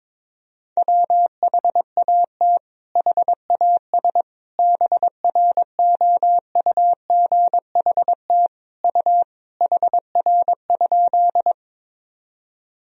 Morse code, heard as a keyed tone, about 22 wpm, 700 Hz. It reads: WHAT HAS BROUGHT U HR?
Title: Sentences from Top 500 Words and Abbreviations 22wpm